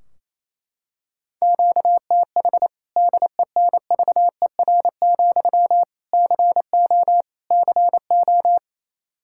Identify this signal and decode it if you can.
Morse code, heard as a keyed tone, about 28 wpm, 700 Hz. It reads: QTH DENVER, CO CO
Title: Practice QSO - Exchange 1 - Beginning - 28wpm